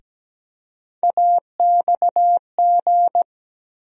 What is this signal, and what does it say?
Morse code, heard as a keyed tone, about 17 wpm, 700 Hz. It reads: AXG
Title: USA Call Sign Postfixes - 3 Letters 17wpm